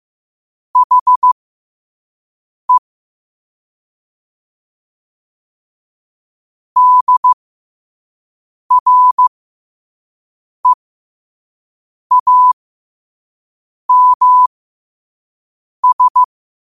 Morse code, heard as a keyed tone, about 15 wpm, spaced as for 5 wpm, 1000 Hz. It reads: HE DREAMS
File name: qotdfiles / 5wpm